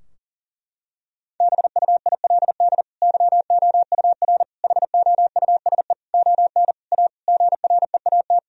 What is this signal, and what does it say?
Morse code, heard as a keyed tone, about 40 wpm, 700 Hz. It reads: BUILD YOUR HOUSE ON A GREAT
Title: Sentences from Top 1000 Words - Review Part 2 - 40wpm